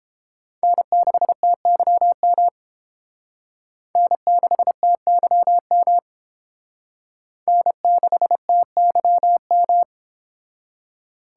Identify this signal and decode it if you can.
Morse code, heard as a keyed tone, about 28 wpm, 700 Hz. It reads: N6TYM N6TYM N6TYM